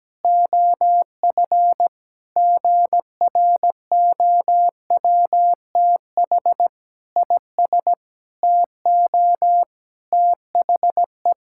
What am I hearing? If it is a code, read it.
Morse code, heard as a keyed tone, about 17 wpm, 700 Hz. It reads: OF GROWTH IS TO THE
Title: Sentences from Top 1000 Words - Review Part 5 - 17wpm